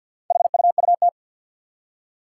Morse code, heard as a keyed tone, about 46 wpm, 700 Hz. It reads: HUUT
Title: Four Challenging Letter-Number Combinations 50wpm